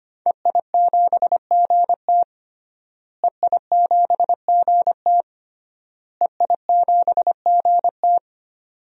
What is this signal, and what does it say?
Morse code, heard as a keyed tone, about 25 wpm, 700 Hz. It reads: EI7GT EI7GT EI7GT
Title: DXCC-100 Call Signs - 5 Characters - Repeated 3x - 25wpm